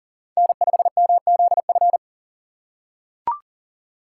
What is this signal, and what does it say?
Morse code, heard as a keyed tone, about 40 wpm, 700 Hz. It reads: N5MZF
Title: USA Call Signs - 5 Characters - 3x Character Spacing - 40wpm